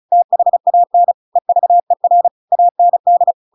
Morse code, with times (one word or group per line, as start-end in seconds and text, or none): 0.1-1.1 THAN
1.4-2.3 EVER
2.5-3.3 AND